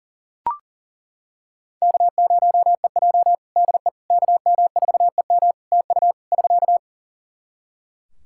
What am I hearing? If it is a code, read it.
Morse code, heard as a keyed tone, about 41 wpm, 700 Hz. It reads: K0EJ DE KM4EM TU <SK>
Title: Practice QSO - Exchange 4 - Ending - 40wpm